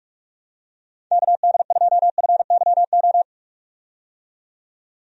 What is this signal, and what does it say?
Morse code, heard as a keyed tone, about 41 wpm, 700 Hz. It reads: KD2FYO
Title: USA Call Signs - 6 Characters - 45wpm